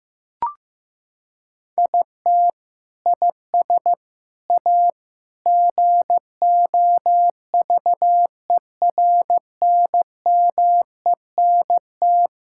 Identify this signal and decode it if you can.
Morse code, heard as a keyed tone, about 14 wpm, 700 Hz. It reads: IT IS A GOVERNMENT
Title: Sentences from Top 1000 Words - Review Part 5 - 15wpm